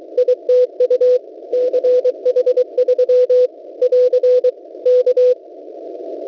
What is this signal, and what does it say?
Morse code, heard as a keyed tone, about 23 wpm, 500 Hz. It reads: ITU CH3 <AR> K
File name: WebSDR records / w7pua-2.ddns.net 6477.50 cw